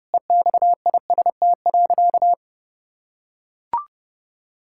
Morse code, heard as a keyed tone, about 32 wpm, 700 Hz. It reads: EXIST.
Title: Sentences from Top 1000 Words - Review Part 5 - 30wpm